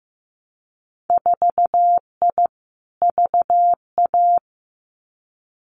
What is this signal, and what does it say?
Morse code, heard as a keyed tone, about 15 wpm, 700 Hz. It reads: 4I VA
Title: Winter Field Day Exchanges 15wpm